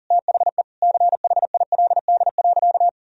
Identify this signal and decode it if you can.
Morse code, heard as a keyed tone, about 40 wpm, 700 Hz. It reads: THE CHILD.
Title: Sentences from Top 1000 Words - Review Part 1 - 40wpm